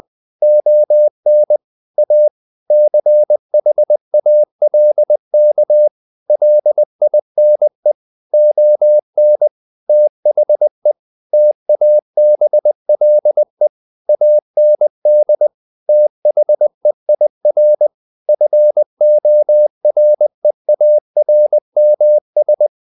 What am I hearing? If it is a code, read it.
Morse code, heard as a keyed tone, about 20 wpm, 600 Hz. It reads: ON A CHALK LINE ON THE TABLE AND THEIR FOREARMS
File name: the old man and the sea / the old man and the sea part 1 20wpm